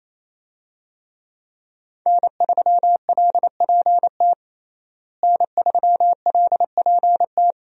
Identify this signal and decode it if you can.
Morse code, heard as a keyed tone, about 28 wpm, 700 Hz. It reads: N3LPT N3LPT
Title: USA Call Signs - 5 Characters - Repeated 2x - 28wpm